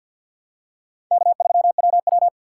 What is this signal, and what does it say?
Morse code, heard as a keyed tone, about 52 wpm, 700 Hz. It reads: K3WW
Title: USA Call Signs - 4 Characters - 6x Character Spacing - 50wpm